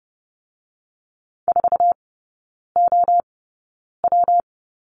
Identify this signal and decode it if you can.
Morse code, heard as a keyed tone, about 30 wpm, 700 Hz. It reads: 4 O W
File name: Lesson 19e - Letter w - Three Characters 30wpm